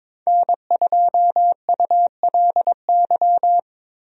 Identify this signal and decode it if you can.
Morse code, heard as a keyed tone, about 22 wpm, 700 Hz. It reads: N2ULY
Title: USA Call Signs - 5 Characters - 22wpm